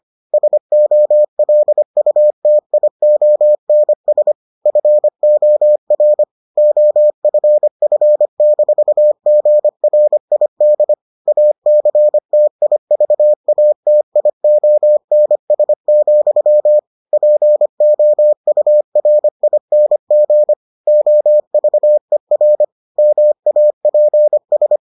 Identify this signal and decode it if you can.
Morse code, heard as a keyed tone, about 25 wpm, 600 Hz. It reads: SOLUTIONS FOR OFF-GRID ACTIVATIONS, POURING OVER MAPS